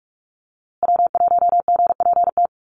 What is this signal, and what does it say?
Morse code, heard as a keyed tone, about 44 wpm, 700 Hz. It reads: W1GPT
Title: USA Call Signs Repeated 3x 45wpm